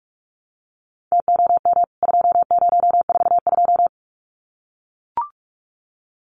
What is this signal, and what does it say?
Morse code, heard as a keyed tone, about 44 wpm, 700 Hz. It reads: TOM 2042
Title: CWT Exchange 45wpm